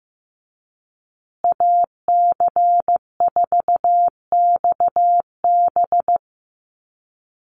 Morse code, heard as a keyed tone, about 15 wpm, 700 Hz. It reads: AC4XB
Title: USA Call Signs Repeated 2x 15wpm